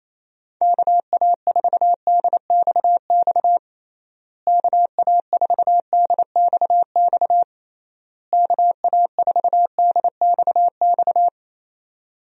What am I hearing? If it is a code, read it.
Morse code, heard as a keyed tone, about 28 wpm, 700 Hz. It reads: KA4DXX KA4DXX KA4DXX
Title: USA Call Signs - 6 Characters - Repeated 3x - 28wpm